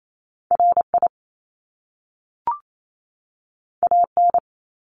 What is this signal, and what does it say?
Morse code, heard as a keyed tone, about 28 wpm, 700 Hz. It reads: RI AN
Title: Top Two Letter Combinations in Words 28wpm